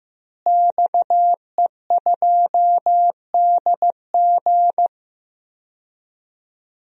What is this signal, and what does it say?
Morse code, heard as a keyed tone, about 16 wpm, 700 Hz. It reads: XE2DG
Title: DXCC-100 Call Signs - 5 Characters - Repeated 2x - 15wpm